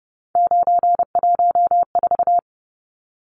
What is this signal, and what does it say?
Morse code, heard as a keyed tone, about 30 wpm, 700 Hz. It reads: 914
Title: Numbers - 3-Digit 30wpm